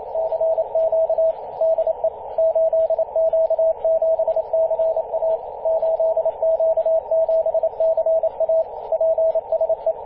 Morse code, heard as a keyed tone, about 29 wpm, 650 Hz, with radio noise. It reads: CQ DE 8Q7CA 8Q7CA PSE
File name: 8q7ca-21-013mhz-cw